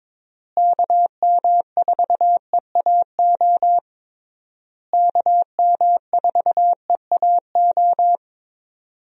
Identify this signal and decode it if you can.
Morse code, heard as a keyed tone, about 22 wpm, 700 Hz. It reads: KM4EAO KM4EAO
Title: USA Call Signs - 6 Characters - Repeated 3x - 22wpm